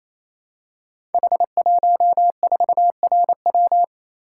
Morse code, heard as a keyed tone, about 28 wpm, 700 Hz. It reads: H14RW
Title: Code Groups - 5 Letter:Numbers - 28wpm